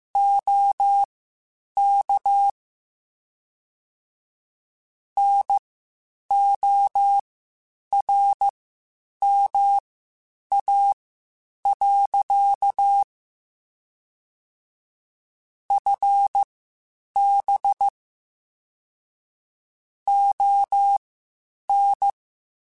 Morse code, heard as a keyed tone, about 15 wpm, spaced as for 8 wpm, 800 Hz. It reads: OK NORMA. FB ON
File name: (spaced) / QSOs Spaced-mp3s / WD0FBT 1 @13 sp